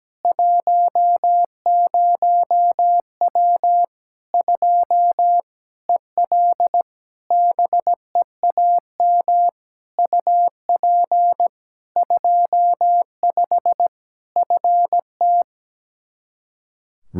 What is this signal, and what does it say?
Morse code, heard as a keyed tone, about 17 wpm, 700 Hz. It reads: 10W 2 EL BEAM UP 25 FT
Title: Practice QSO - Exchange 2 - Info - 17wpm